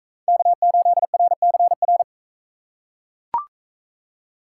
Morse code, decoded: K8RCR

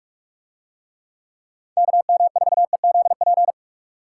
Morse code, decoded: KM4EZP

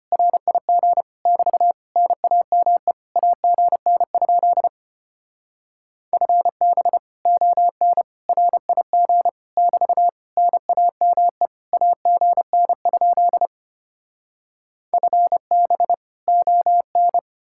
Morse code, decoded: RIG <BT> NAME AGN? FB ON RIG <BT> NAME AGN? FB ON

